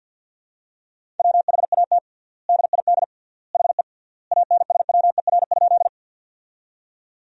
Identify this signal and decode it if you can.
Morse code, heard as a keyed tone, about 50 wpm, 700 Hz. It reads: WHAT DID HE ANSWER?
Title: 3x Word-Spacing - Sentences from Top 300 Words - 50wpm